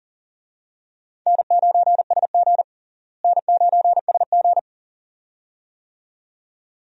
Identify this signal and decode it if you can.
Morse code, heard as a keyed tone, about 40 wpm, 700 Hz. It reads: N9SG N9SG
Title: USA Call Signs - 4 Characters - Repeated 2x - 40wpm